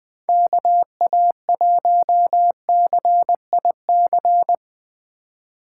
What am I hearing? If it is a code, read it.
Morse code, heard as a keyed tone, about 21 wpm, 700 Hz. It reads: KA1CIC